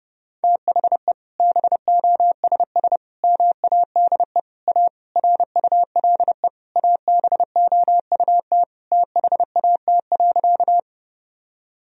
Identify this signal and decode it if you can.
Morse code, heard as a keyed tone, about 30 wpm, 700 Hz. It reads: THE BOSS MADE A RULE ABOUT THAT.
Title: Sentences from Top 1000 Words - Review Part 4 - 30wpm